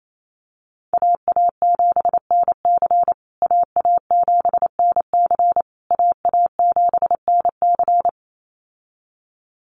Morse code, heard as a keyed tone, about 28 wpm, 700 Hz. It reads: AA7NC AA7NC AA7NC
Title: USA Call Signs Repeated 3x 28wpm